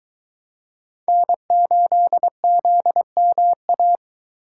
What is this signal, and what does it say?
Morse code, heard as a keyed tone, about 22 wpm, 700 Hz. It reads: N8ZMA